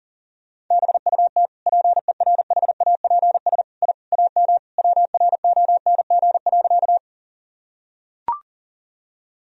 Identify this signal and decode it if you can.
Morse code, heard as a keyed tone, about 40 wpm, 700 Hz. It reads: BUT PERHAPS I AM WRONG.
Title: Sentences from Top 1000 Words - Review Part 2 - 40wpm